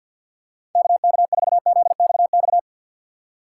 Morse code, heard as a keyed tone, about 53 wpm, 700 Hz. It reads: KK4ZXX
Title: USA Call Signs - 4x Character Spacing - 50wpm